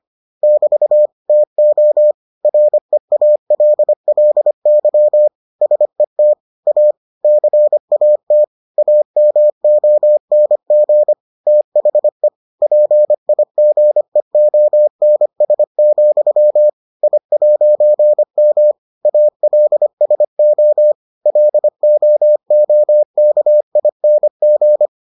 Morse code, heard as a keyed tone, about 25 wpm, 600 Hz. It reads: <BT> TO REALLY SET A CAT AMONG THE PIGEONS, I'M ALSO LOOKING